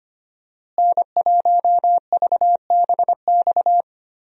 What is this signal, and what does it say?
Morse code, heard as a keyed tone, about 25 wpm, 700 Hz. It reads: N1VBX
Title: USA Call Signs - 5 Characters - Repeated 2x - 25wpm